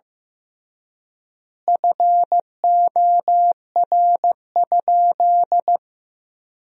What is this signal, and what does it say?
Morse code, heard as a keyed tone, about 15 wpm, 700 Hz. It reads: FOR?
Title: 3x Word-Spacing - Sentences from Top 100 Words - 15wpm